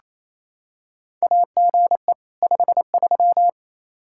Morse code, heard as a keyed tone, about 27 wpm, 700 Hz. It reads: AGE 53
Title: K1USN SST QSO Practice 28wpm